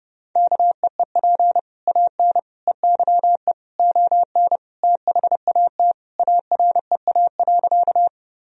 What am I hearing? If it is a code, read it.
Morse code, heard as a keyed tone, about 30 wpm, 700 Hz. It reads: KEEP AN EYE ON THAT AREA.
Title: Sentences from Top 1000 Words - Review Part 2 - 30wpm